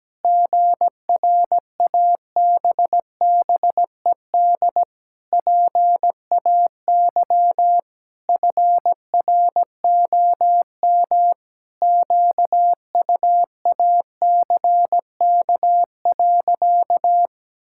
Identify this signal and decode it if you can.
Morse code, heard as a keyed tone, about 17 wpm, 700 Hz. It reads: GRABBED PAY FROM QUACK.
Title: Pangram Challenge v2 - 17wpm